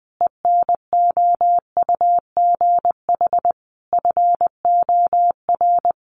Morse code, heard as a keyed tone, about 20 wpm, 700 Hz. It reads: ENOUGH FOR
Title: Sentences from Top 900 Words 20wpm